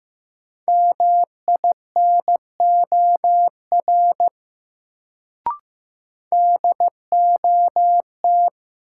Morse code, heard as a keyed tone, about 15 wpm, 700 Hz. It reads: MINOR DOT
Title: Sentences from Top 1500 Words - New Word Drill Alt - 15wpm